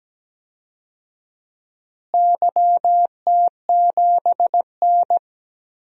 Morse code, heard as a keyed tone, about 17 wpm, 700 Hz. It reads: YT7N
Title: DXCC-100 Call Signs - 4 Characters - Repeated 3x - 17wpm